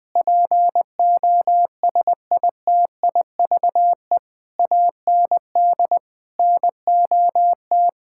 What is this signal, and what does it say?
Morse code, heard as a keyed tone, about 20 wpm, 700 Hz. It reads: POSITIVE AND NOT
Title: Sentences from Top 1000 Words - Review Part 6 - 20wpm